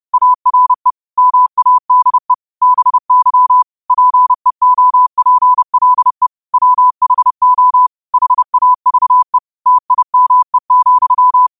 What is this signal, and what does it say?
Morse code, heard as a keyed tone, about 30 wpm, 1000 Hz. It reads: ARE MADE BY PEOPLE WHO HAVE TIME,
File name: qotdfiles / 30wpm